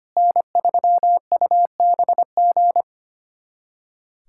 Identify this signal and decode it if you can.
Morse code, heard as a keyed tone, about 25 wpm, 700 Hz. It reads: N3UBG
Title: USA Callsign DE Callsign - 5 Characters - 25wpm